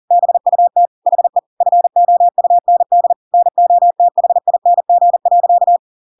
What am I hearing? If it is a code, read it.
Morse code, heard as a keyed tone, about 40 wpm, 700 Hz. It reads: BUT HE FOUND NOTHING.
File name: Pinocchio / Pinocchio ch5 40 40 ik1hge